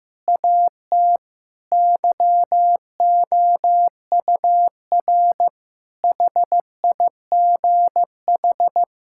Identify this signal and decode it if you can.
Morse code, heard as a keyed tone, about 15 wpm, 700 Hz. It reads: AT YOUR HIGH
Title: Sentences from Top 1000 Words - Review Part 6 - 15wpm